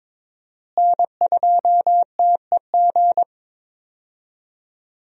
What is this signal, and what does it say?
Morse code, heard as a keyed tone, about 21 wpm, 700 Hz. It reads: N2TEG
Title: USA Call Signs - 5 Characters - Repeated 2x - 22wpm